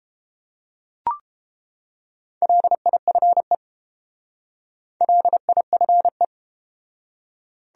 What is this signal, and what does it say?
Morse code, heard as a keyed tone, about 31 wpm, 700 Hz. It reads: LIFE LIFE